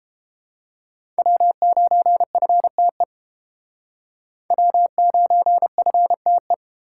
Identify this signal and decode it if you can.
Morse code, heard as a keyed tone, about 30 wpm, 700 Hz. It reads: W9FTE W9FTE